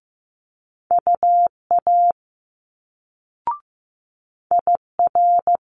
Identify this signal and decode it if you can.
Morse code, heard as a keyed tone, about 15 wpm, 700 Hz. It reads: UA IR
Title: Multi-letter Phonemes Quick Fire 15wpm